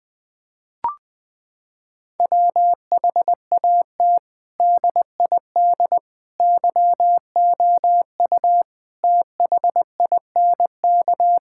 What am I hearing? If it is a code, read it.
Morse code, heard as a keyed tone, about 20 wpm, 700 Hz. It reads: WHAT DID YOU THINK
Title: Sentences from Top 1000 Words - Review Part 6 - 20wpm